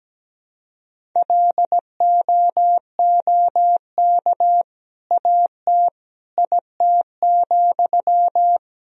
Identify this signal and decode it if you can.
Morse code, heard as a keyed tone, about 17 wpm, 700 Hz. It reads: LOOK AT IT,
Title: Sentences from Top 200 Words and Abbreviations 17wpm